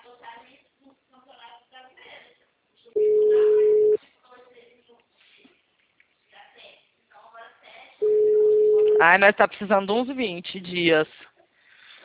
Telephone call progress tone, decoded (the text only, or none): ringback tone